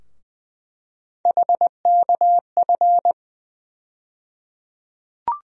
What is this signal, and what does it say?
Morse code, heard as a keyed tone, about 21 wpm, 700 Hz. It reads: HKF
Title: USA Call Sign Postfixes - 3 Letters 20wpm